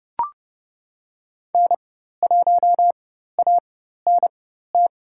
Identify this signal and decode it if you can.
Morse code, heard as a keyed tone, about 30 wpm, spaced as for 14 wpm, 700 Hz. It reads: N1ANT
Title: USA Call Signs - 5 Characters - 4x Character Spacing - 30wpm